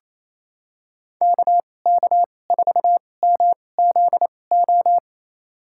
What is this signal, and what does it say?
Morse code, heard as a keyed tone, about 28 wpm, spaced as for 20 wpm, 700 Hz. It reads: KK4MZO
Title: USA Call Signs - 2x Character Spacing - 28wpm